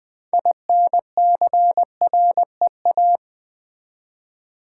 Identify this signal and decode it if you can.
Morse code, heard as a keyed tone, about 20 wpm, 700 Hz. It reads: INCREA
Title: Build A Word - 3 Letters At A Time - 8-Letter Words - 20wpm